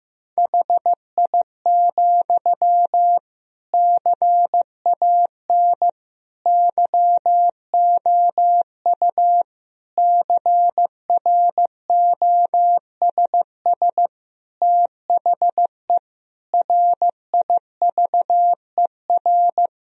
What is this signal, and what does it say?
Morse code, heard as a keyed tone, about 15 wpm, 700 Hz. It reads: HI, CAN YOU CROSS THE RIVER
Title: Sentences from Top 1000 Words - Review Part 3 - 15wpm